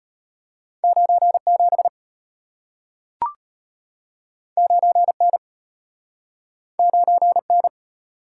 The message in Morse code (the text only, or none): 97 9N 9N